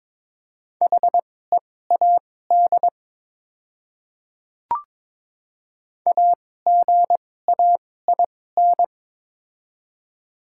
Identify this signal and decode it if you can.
Morse code, heard as a keyed tone, about 22 wpm, spaced as for 16 wpm, 700 Hz. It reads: HEAD AGAIN
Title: 2x Character Spacing - Top 300 Words - 22wpm